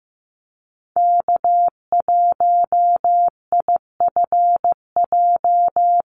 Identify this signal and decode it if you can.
Morse code, heard as a keyed tone, about 15 wpm, 700 Hz. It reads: K1IFJ
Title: Call Signs Repeated 3x - 15wpm